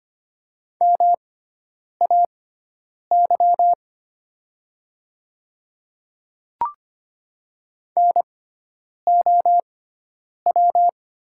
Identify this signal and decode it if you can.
Morse code, heard as a keyed tone, about 25 wpm, spaced as for 9 wpm, 700 Hz. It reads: MAY NOW
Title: 6x Character Spacing - Top 200 Words - 25wpm